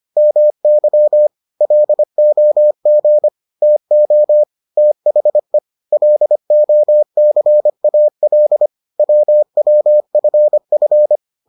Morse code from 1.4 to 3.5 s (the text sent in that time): LOG